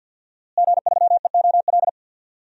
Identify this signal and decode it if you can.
Morse code, heard as a keyed tone, about 48 wpm, 700 Hz. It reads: G3EOL
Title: DXCC-100 Call Signs - 5 Characters - Repeated 3x - 50wpm